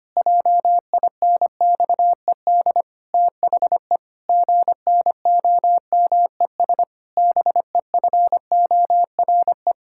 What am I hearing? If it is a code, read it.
Morse code, heard as a keyed tone, about 25 wpm, 700 Hz. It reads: JINXED THE GNOMES BEFORE
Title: Pangram Challenge v2 - 25wpm